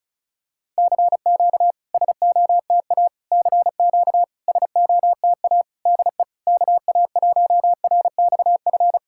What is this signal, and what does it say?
Morse code, heard as a keyed tone, about 37 wpm, 700 Hz. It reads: CQ SOTA CQ SOTA DE KA1RXF
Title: SOTA QSO Practice 35wpm